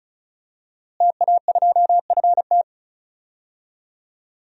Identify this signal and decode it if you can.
Morse code, heard as a keyed tone, about 34 wpm, 700 Hz. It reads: TA2FT
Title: DXCC-100 Call Signs - 5 Characters - Repeated 2x - 35wpm